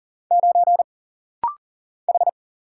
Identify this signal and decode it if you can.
Morse code, heard as a keyed tone, about 40 wpm, 700 Hz. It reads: 9 H
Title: Single Letter-Number Alt - Rapid-Fire 40wpm